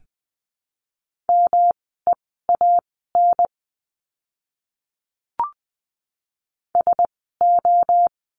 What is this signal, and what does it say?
Morse code, heard as a keyed tone, about 20 wpm, spaced as for 15 wpm, 700 Hz. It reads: MEAN SO
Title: 230 Lesson 37f - Letter x - Words 20wpm